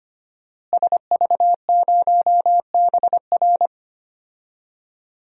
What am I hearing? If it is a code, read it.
Morse code, heard as a keyed tone, about 25 wpm, 700 Hz. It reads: SV0BR
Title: DXCC-100 Call Signs - 5 Characters - Repeated 3x - 25wpm